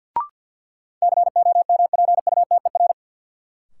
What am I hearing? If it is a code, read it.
Morse code, heard as a keyed tone, about 51 wpm, 700 Hz. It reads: COMPUTER
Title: Sentences from Top 1000 Words - Review Part 3 - New Word Drill - 50wpm